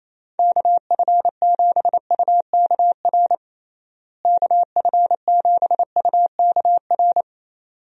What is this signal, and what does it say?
Morse code, heard as a keyed tone, about 28 wpm, 700 Hz. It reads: KF7UKR KF7UKR
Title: USA Call Signs - 6 Characters - Repeated 2x - 28wpm